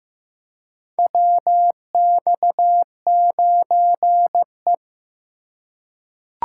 Morse code, heard as a keyed tone, about 15 wpm, 700 Hz. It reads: WX9E
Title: USA Call Signs - 4 Characters - 3x Character Spacing - 15wpm